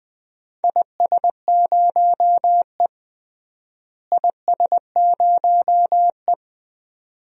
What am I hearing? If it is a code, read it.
Morse code, heard as a keyed tone, about 20 wpm, 700 Hz. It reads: IS0E IS0E
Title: DXCC-100 Call Signs - 4 Characters - Repeated 2x - 20wpm